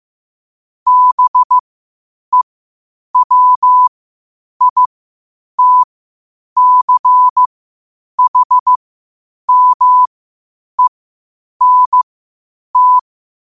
Morse code, heard as a keyed tone, about 15 wpm, spaced as for 8 wpm, 1000 Hz. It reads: BEWITCHMENT
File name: 8wpm